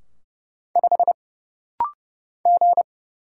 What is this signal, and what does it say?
Morse code, heard as a keyed tone, about 30 wpm, 700 Hz. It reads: 5 G
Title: Single Letter-Number - Rapid-Fire 30wpm